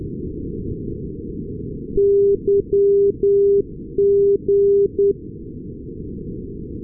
Morse code, heard as a keyed tone, about 10 wpm, 400 Hz. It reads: YG
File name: europe / nor / nor-258 yg-396 la6lu-2020-12-kiwi regional-rb